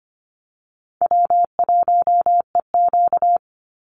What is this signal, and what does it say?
Morse code, heard as a keyed tone, about 25 wpm, 700 Hz. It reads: W1EQ
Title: Easy Call Signs Repeated 3x - 25wpm